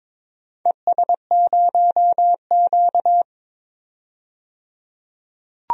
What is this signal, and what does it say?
Morse code, heard as a keyed tone, about 22 wpm, 690 Hz. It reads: ES0Q
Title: DXCC-100 Call Signs - 4 Characters - 22wpm